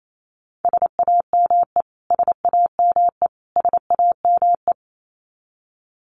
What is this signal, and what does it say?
Morse code, heard as a keyed tone, about 28 wpm, 700 Hz. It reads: SAME SAME SAME
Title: Top 500 Words Repeated 28wpm